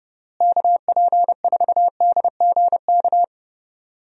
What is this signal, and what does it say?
Morse code, heard as a keyed tone, about 30 wpm, 700 Hz. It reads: KP4DGK
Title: USA Call Signs - 6 Characters - Repeated 2x - 30wpm